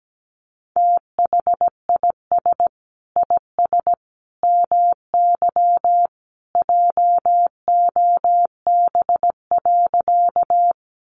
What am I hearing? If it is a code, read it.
Morse code, heard as a keyed tone, about 17 wpm, 700 Hz. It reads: THIS IS MY JOB.
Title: Sentences from Top 600 Words 17wpm